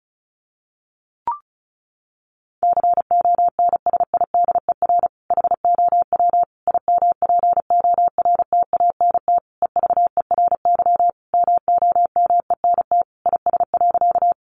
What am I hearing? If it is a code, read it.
Morse code, heard as a keyed tone, about 34 wpm, 700 Hz. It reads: CONSIDER HOW IMPORTANT EVERY MOMENT IS.
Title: Sentences from Top 1000 Words 35wpm